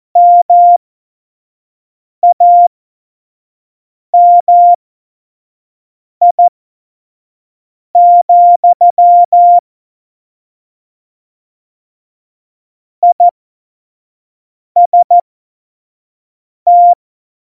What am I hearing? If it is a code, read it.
Morse code, heard as a keyed tone, about 14 wpm, spaced as for 5 wpm, 700 Hz. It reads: MAMI, IST
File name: Morsekurs / Witze / Witz 19. Februar 2019, 25 F70